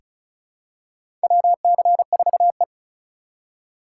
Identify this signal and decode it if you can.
Morse code, heard as a keyed tone, about 36 wpm, 700 Hz. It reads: WC4E
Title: USA Call Signs - 4 Characters - 2x Character Spacing - 35wpm